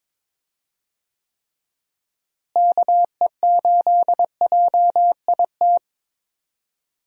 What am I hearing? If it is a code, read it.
Morse code, heard as a keyed tone, about 22 wpm, 700 Hz. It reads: KE8JIT
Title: USA Call Signs - 6 Characters - Repeated 2x - 22wpm